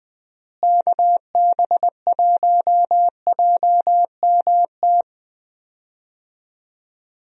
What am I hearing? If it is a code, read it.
Morse code, heard as a keyed tone, about 20 wpm, 700 Hz. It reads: KB1JMT
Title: USA Call Signs - 6 Characters - Repeated 3x - 20wpm